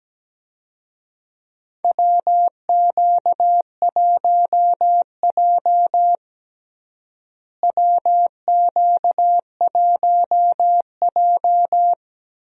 Morse code, heard as a keyed tone, about 17 wpm, 700 Hz. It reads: WQ1J WQ1J
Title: USA Call Signs - 4 Characters - Repeated 3x - 17wpm